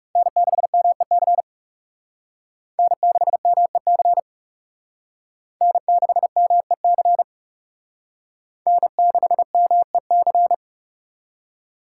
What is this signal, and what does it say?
Morse code, heard as a keyed tone, about 37 wpm, 700 Hz. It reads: N6MEC N6MEC N6MEC N6MEC
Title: Speed Racing - USA 1x3 Call Signs 30wpm